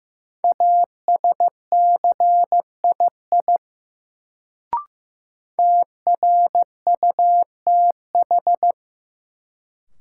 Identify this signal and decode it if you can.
Morse code, heard as a keyed tone, about 15 wpm, 700 Hz. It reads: ASCII TRUTH
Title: Sets of 5 Words - New Word Drill - 15wpm